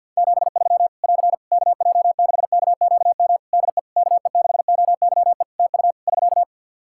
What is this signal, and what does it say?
Morse code, heard as a keyed tone, about 50 wpm, 700 Hz. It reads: 73 <AR> KJ6KQM DE KE6OYE TU <SK>
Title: Practice QSO - Exchange 4 - Ending - 50wpm